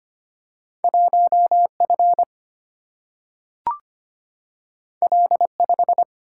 Morse code, heard as a keyed tone, about 25 wpm, 700 Hz. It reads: 1 F L 5
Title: Single-Letter Single-Number 25wpm